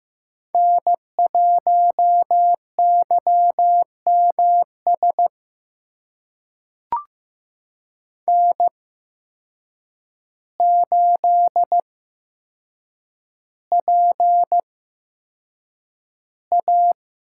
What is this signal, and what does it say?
Morse code, heard as a keyed tone, about 15 wpm, spaced as for 7 wpm, 700 Hz. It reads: N1YMS N8PA
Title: USA Call Signs - 5 Characters - 8x Character Spacing - 15wpm